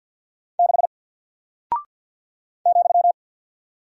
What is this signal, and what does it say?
Morse code, heard as a keyed tone, about 50 wpm, 700 Hz. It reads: 6 ,
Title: Single Character (Letter:Number:Punctuation:Prosign) 50wpm